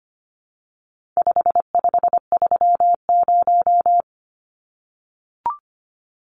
Morse code, heard as a keyed tone, about 25 wpm, 700 Hz. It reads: 5530
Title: Numbers - 4-Digit 25wpm